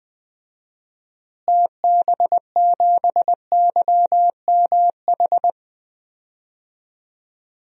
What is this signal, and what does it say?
Morse code, heard as a keyed tone, about 21 wpm, 700 Hz. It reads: TB7YMH
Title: DXCC-100 Call Signs - 6 Characters - Repeated 3x - 20wpm